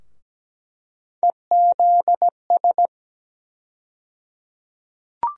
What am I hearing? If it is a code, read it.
Morse code, heard as a keyed tone, about 17 wpm, 700 Hz. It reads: EZS
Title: USA Call Sign Postfixes - 3 Letters 17wpm